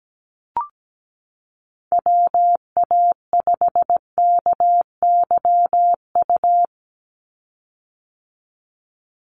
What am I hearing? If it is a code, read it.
Morse code, heard as a keyed tone, about 17 wpm, 700 Hz. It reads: WA5KYU
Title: USA Call Signs 17wpm